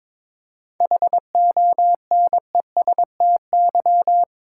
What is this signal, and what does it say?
Morse code, heard as a keyed tone, about 22 wpm, 700 Hz. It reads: HONESTY
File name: Difficult Sentences with Hints - 22wpm